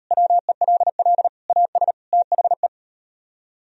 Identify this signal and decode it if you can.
Morse code, heard as a keyed tone, about 38 wpm, 700 Hz. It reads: WELL AS THE